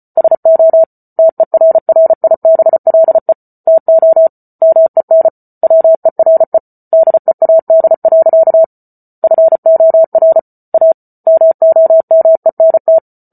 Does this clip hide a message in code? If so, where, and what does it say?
Yes: Morse code from 0.2 to 13.0 s, heard as a keyed tone, about 34 wpm, 650 Hz. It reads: SO TERRIBLE TO MEN WERE DEAD. FOR A MOMENT